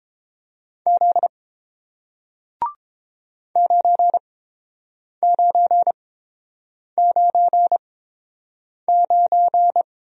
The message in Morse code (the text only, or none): Z 9999